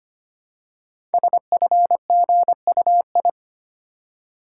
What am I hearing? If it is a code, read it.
Morse code, heard as a keyed tone, about 25 wpm, 700 Hz. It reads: SFGUI
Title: Code Groups - 5 Letter:Numbers - 25wpm